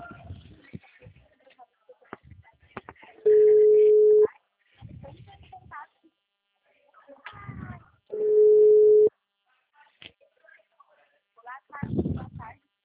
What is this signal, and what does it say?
Ringback tone (Europe, 425 Hz), with speech.